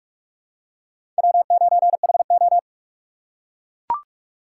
Morse code, heard as a keyed tone, about 44 wpm, 700 Hz. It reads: W9HO
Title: USA Call Signs - 4 Characters - 6x Character Spacing - 45wpm